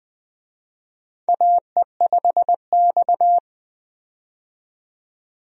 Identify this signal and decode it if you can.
Morse code, heard as a keyed tone, about 20 wpm, 700 Hz. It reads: AE5X
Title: USA Call Signs - 4 Characters - 20wpm